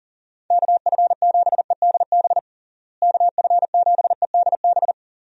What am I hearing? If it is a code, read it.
Morse code, heard as a keyed tone, about 37 wpm, 700 Hz. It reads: KF7EDB KF7EDB
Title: USA Call Signs - 6 Characters - Repeated 2x - 40wpm